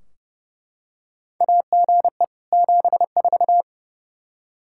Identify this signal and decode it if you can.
Morse code, heard as a keyed tone, about 32 wpm, 700 Hz. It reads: AGE 74
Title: Practice QSO - Exchange 3 - More Info - 30wpm